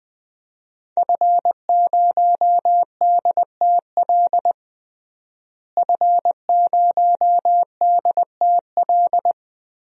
Morse code, heard as a keyed tone, about 20 wpm, 700 Hz. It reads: F0DTL F0DTL
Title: DXCC-100 Call Signs - 5 Characters - Repeated 3x - 20wpm